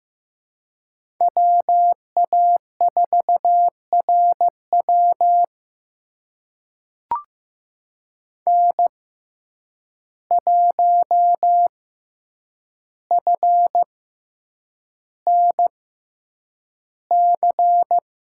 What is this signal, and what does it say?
Morse code, heard as a keyed tone, about 15 wpm, spaced as for 8 wpm, 700 Hz. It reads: WA4RW N1FNC
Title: USA Call Signs - 5 Characters - 6x Character Spacing - 15wpm